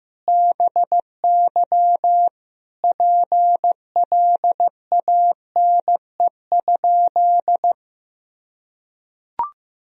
Morse code, heard as a keyed tone, about 15 wpm, 700 Hz. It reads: BY PLANE?
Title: Sentences from Top 1000 Words - Review Part 3 - 15wpm